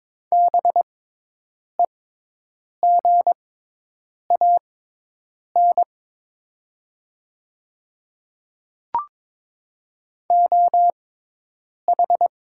Morse code, heard as a keyed tone, about 22 wpm, spaced as for 8 wpm, 700 Hz. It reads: BEGAN OH
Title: 6x Character Spacing - Top 500 Words - 22wpm